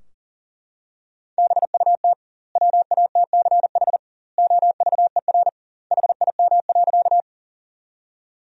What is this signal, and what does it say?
Morse code, heard as a keyed tone, about 40 wpm, 700 Hz. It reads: BUT WATCH OVER HIM.
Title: 2x Word-Spacing - Sentences from Top 300 Words - 40wpm